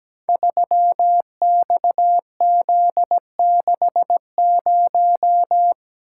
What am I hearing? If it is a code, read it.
Morse code, heard as a keyed tone, about 17 wpm, 700 Hz. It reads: 3XZ60
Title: Code Groups - 5 Letter:Numbers - 17wpm